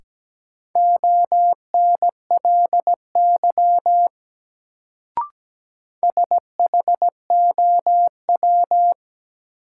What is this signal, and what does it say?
Morse code, heard as a keyed tone, about 17 wpm, 700 Hz. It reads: ONLY SHOW
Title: Sentences from Top 200 Words - New Word Drill - 17wpm